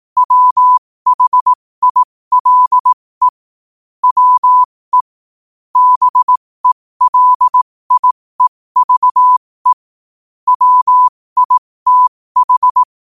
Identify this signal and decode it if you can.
Morse code, heard as a keyed tone, about 18 wpm, spaced as for 15 wpm, 1000 Hz. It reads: WHILE WE BELIEVE WITH
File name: qotdfiles / 15wpm